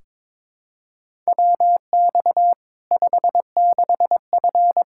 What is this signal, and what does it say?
Morse code, heard as a keyed tone, about 22 wpm, 700 Hz. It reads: WX 56F
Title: Practice QSO - Exchange 2 - Info - 22wpm